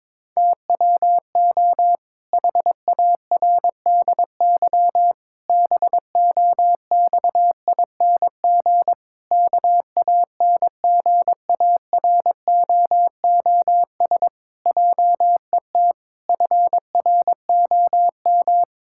Morse code, heard as a keyed tone, about 22 wpm, 700 Hz. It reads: TWO HARDY BOXING KANGAROOS JET FROM
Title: Pangram Challenge v2 - 22wpm